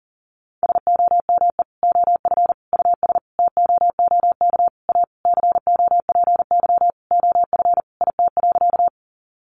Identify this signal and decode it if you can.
Morse code, heard as a keyed tone, about 40 wpm, 700 Hz. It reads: SOME OF US TOOK A COPY OF IT.
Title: Sentences from Top 1400 Words 40wpm